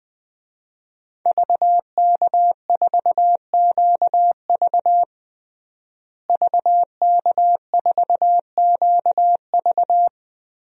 Morse code, heard as a keyed tone, about 20 wpm, 700 Hz. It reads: VK4QV VK4QV
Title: DXCC-100 Call Signs - 5 Characters - Repeated 3x - 20wpm